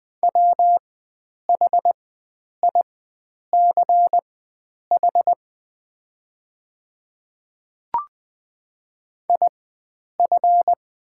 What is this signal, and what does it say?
Morse code, heard as a keyed tone, about 20 wpm, spaced as for 9 wpm, 700 Hz. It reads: WHICH IF
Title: 4x Character Spacing - Top 100 Words - 20wpm